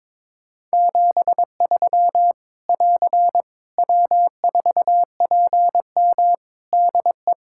Morse code, heard as a keyed tone, about 22 wpm, 700 Hz. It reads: 73 <AR> W4PM DE
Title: K1USN SST QSO Practice 22wpm